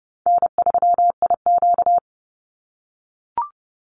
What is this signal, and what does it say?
Morse code, heard as a keyed tone, about 30 wpm, 700 Hz. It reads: N3IQ
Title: Call Signs - 30wpm